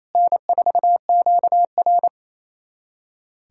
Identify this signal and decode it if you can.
Morse code, heard as a keyed tone, about 28 wpm, 700 Hz. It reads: N4QR
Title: USA Callsign DE Callsign - 4 Characters - 28wpm